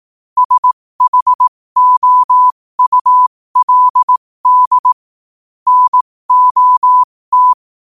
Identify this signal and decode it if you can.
Morse code, heard as a keyed tone, about 18 wpm, spaced as for 15 wpm, 1000 Hz. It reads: SHOULD NOT
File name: qotdfiles / 15wpm